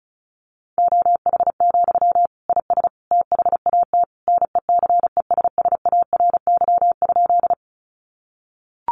Morse code, heard as a keyed tone, about 34 wpm, 700 Hz. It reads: OH, IS THAT NECESSARY?
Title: Sentences from Top 800 Words 35wpm